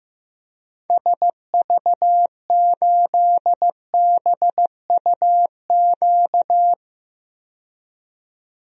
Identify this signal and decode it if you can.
Morse code, heard as a keyed tone, about 15 wpm, 700 Hz. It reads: SV8BUQ
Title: DXCC-100 Call Signs - 6 Characters - 15wpm